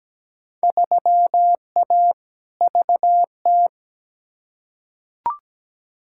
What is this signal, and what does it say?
Morse code, heard as a keyed tone, about 17 wpm, 700 Hz. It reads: 3A VT
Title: Field Day Exchanges Encore - 17wpm